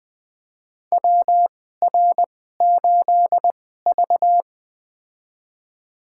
Morse code, heard as a keyed tone, about 20 wpm, spaced as for 15 wpm, 700 Hz. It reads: WR8V
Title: USA Call Signs - 4 Characters - 2x Character Spacing - 20wpm